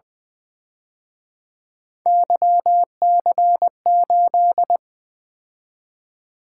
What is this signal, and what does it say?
Morse code, heard as a keyed tone, about 24 wpm, 700 Hz. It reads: YC8
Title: Speed-Racing - Three Letter-Number Combinations - 20wpm